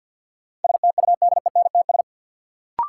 Morse code, heard as a keyed tone, about 49 wpm, 700 Hz. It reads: STUDENTS
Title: Sets of 5 Words Encore - New Word Drill - 50wpm